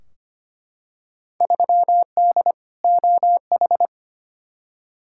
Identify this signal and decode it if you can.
Morse code, heard as a keyed tone, about 25 wpm, 700 Hz. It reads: 3D OH
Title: Field Day Exchanges - 25wpm